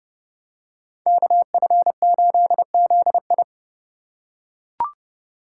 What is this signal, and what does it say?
Morse code, heard as a keyed tone, about 28 wpm, 700 Hz. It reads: KF8ZI